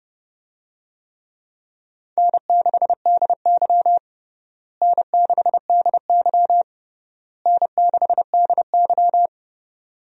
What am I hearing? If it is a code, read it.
Morse code, heard as a keyed tone, about 30 wpm, 700 Hz. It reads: N6DY N6DY N6DY
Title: USA Call Signs - 4 Characters - Repeated 3x - 30wpm